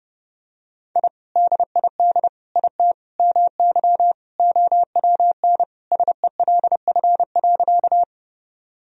Morse code, heard as a keyed tone, about 30 wpm, 700 Hz. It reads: I DID IT MY OWN SELF.
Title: Sentences from Top 1000 Words - Review Part 1 - 30wpm